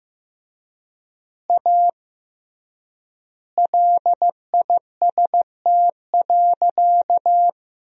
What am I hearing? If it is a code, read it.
Morse code, heard as a keyed tone, about 15 wpm, 700 Hz. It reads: A LIST.
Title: 3x Word-Spacing - Sentences from Top 400 Words - 15wpm